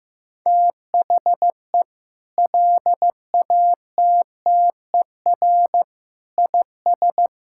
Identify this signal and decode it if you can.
Morse code, heard as a keyed tone, about 15 wpm, 700 Hz. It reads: THE LATTER IS
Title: Sentences from Top 1600 Words 15wpm